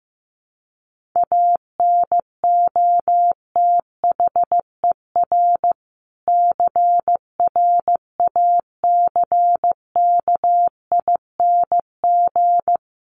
Morse code, heard as a keyed tone, about 15 wpm, 700 Hz. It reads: ANOTHER CRACKING
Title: A Sound Of Thunder 15wpm